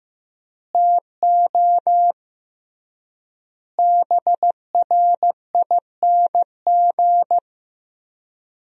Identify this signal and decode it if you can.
Morse code, heard as a keyed tone, about 15 wpm, 700 Hz. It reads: TO BRING
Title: 3x Word-Spacing - Sentences from Top 500 Words - 15wpm